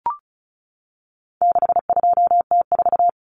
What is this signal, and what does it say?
Morse code, heard as a keyed tone, about 34 wpm, 700 Hz. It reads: 62T4
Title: Common Cut Numbers - 4-Digit 35wpm